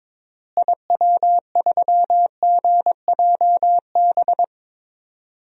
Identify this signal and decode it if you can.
Morse code, heard as a keyed tone, about 22 wpm, 700 Hz. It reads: IW3GJB
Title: DXCC-100 Call Signs - 6 Characters - Repeated 2x - 22wpm